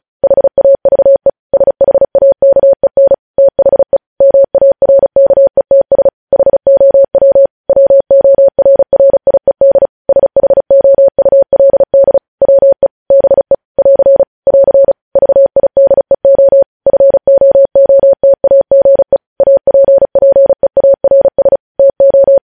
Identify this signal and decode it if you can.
Morse code, heard as a keyed tone, about 36 wpm, 550 Hz. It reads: HAVE SHAKEN THE MARKETS HOW WORRIED SHOULD WE BE <AR> <AR> VIDEO FOOTAGE APPEARS TO
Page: news headlines in morse code at 35 wpm